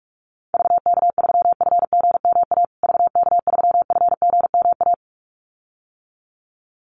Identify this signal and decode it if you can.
Morse code, heard as a keyed tone, about 44 wpm, 700 Hz. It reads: VK3FGMA VK3FGMA
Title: USA Call Signs Repeated 2x 45wpm